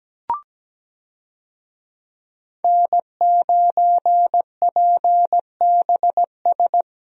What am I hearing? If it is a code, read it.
Morse code, heard as a keyed tone, about 17 wpm, 700 Hz. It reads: N9PBS
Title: USA Call Signs - 5 Characters - 17wpm